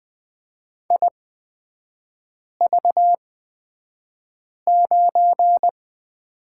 Morse code, heard as a keyed tone, about 20 wpm, spaced as for 5 wpm, 700 Hz. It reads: IV9
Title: 243 Lesson 39e - Letter z - Three Characters 20wpm